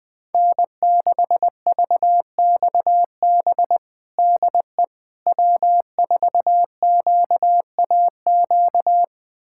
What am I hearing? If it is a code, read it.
Morse code, heard as a keyed tone, about 21 wpm, 700 Hz. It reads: N6VXB DE W4QAQ
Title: USA Callsign DE Callsign - 5 Characters - 20wpm